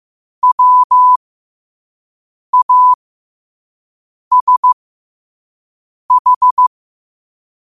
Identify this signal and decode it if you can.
Morse code, heard as a keyed tone, about 15 wpm, spaced as for 5 wpm, 1000 Hz. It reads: WASH
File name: qotdfiles / 5wpm